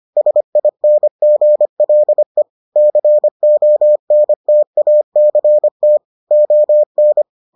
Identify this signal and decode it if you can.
Morse code, heard as a keyed tone, about 25 wpm, 600 Hz. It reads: SINGLE CONTACT ON